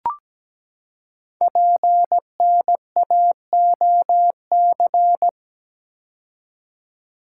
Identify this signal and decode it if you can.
Morse code, heard as a keyed tone, about 17 wpm, 700 Hz. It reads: PNAOC
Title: Code Groups - 5 Letters - 17wpm